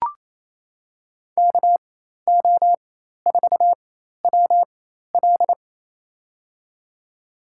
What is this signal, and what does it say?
Morse code, heard as a keyed tone, about 29 wpm, spaced as for 13 wpm, 700 Hz. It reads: KO4WL
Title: USA Call Signs - 5 Characters - 4x Character Spacing - 28wpm